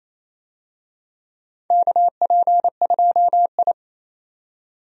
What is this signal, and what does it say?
Morse code, heard as a keyed tone, about 28 wpm, 700 Hz. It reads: KP2I
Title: DXCC-100 Call Signs - 4 Characters - 28wpm